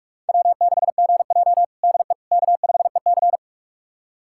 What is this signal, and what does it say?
Morse code, heard as a keyed tone, about 42 wpm, 700 Hz. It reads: W6GJ DE K5EC
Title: USA Callsign DE Callsign - 4 Characters - 45wpm